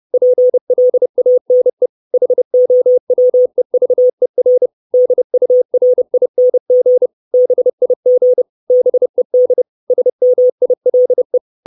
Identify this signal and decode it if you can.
Morse code, heard as a keyed tone, about 30 wpm, 500 Hz. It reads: PLANE HOWEVER DURING BIG BED SMILE
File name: 632 Words 30WPM